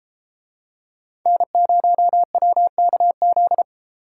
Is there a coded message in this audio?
Yes: Morse code from 1.3 to 3.6 s, heard as a keyed tone, about 33 wpm, 700 Hz. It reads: N0WKZ